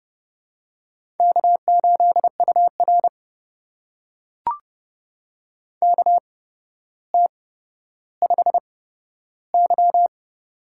Morse code, heard as a keyed tone, about 30 wpm, spaced as for 13 wpm, 700 Hz. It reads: K8UR KT5Y
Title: USA Call Signs - 4 Characters - 8x Character Spacing - 30wpm